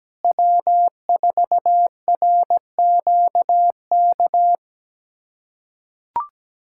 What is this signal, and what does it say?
Morse code, heard as a keyed tone, about 17 wpm, 700 Hz. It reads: W4RQK
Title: USA Callsign DE Callsign - 5 Characters - 17wpm